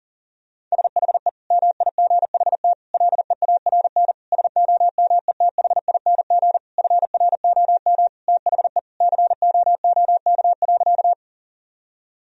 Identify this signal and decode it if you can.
Morse code, heard as a keyed tone, about 40 wpm, 690 Hz. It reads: SHE MIGHT LEARN SOMETHING FROM THE COOK.
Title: Sentences from Top 1000 Words - Review Part 4 - 40wpm